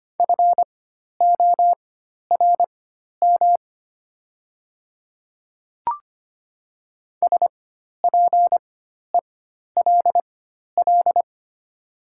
Morse code, heard as a keyed tone, about 25 wpm, spaced as for 12 wpm, 690 Hz. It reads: FORM SPELL
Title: 4x Character Spacing - Top 200 Words - 25wpm